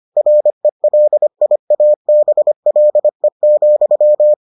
Morse code, heard as a keyed tone, about 25 wpm, 600 Hz. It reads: RELIABLE,